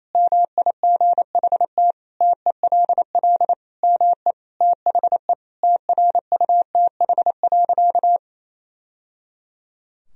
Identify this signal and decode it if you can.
Morse code, heard as a keyed tone, about 28 wpm, 700 Hz. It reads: MIGHT TELL ME THE TRUTH.
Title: Sentences from Top 1000 Words - Review Part 1 - 28wpm